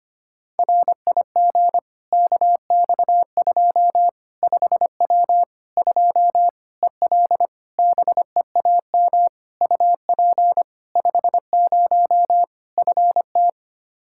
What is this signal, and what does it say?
Morse code, heard as a keyed tone, about 25 wpm, 700 Hz. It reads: RIG KX2 5W 2 EL BEAM UP 50 FT
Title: Practice QSO - Exchange 2 - Info - 25wpm